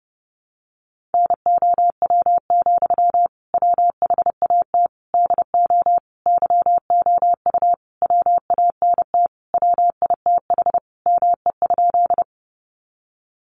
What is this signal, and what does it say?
Morse code, heard as a keyed tone, about 30 wpm, 700 Hz. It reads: NOW, WHAT DO YOU WANT WITH ME?
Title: Sentences from Top 200 Words 30wpm